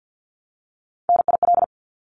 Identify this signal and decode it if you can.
Morse code, heard as a keyed tone, about 50 wpm, 700 Hz. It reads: NIL
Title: Abbreviations 50wpm